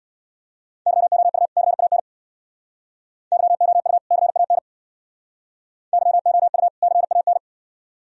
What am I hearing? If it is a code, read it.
Morse code, heard as a keyed tone, about 59 wpm, 700 Hz. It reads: YOU CAN YOU CAN YOU CAN
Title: Speed-Racing - Sets of 2 Words 50wpm